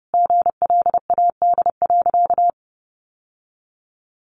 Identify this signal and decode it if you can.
Morse code, heard as a keyed tone, about 30 wpm, 700 Hz. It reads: GLAD.
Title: Sentences from Top 800 Words 30wpm